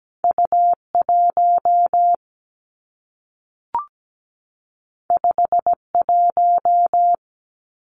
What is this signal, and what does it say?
Morse code, heard as a keyed tone, about 17 wpm, 700 Hz. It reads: U 1 5 1
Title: Two Challenging Letter-Number Combinations 17wpm